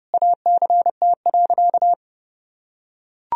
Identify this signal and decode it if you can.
Morse code, heard as a keyed tone, about 30 wpm, 700 Hz. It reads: ACT.
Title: Sentences from Top 1000 Words - Review Part 3 - 30wpm